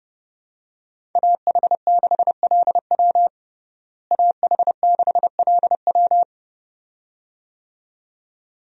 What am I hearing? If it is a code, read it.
Morse code, heard as a keyed tone, about 32 wpm, 700 Hz. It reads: AH6LW AH6LW
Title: USA Call Signs - 5 Characters - Repeated 2x - 30wpm